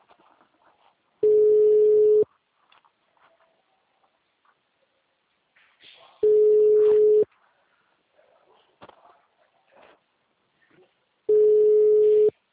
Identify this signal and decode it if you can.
Ringback tone (Europe, 425 Hz).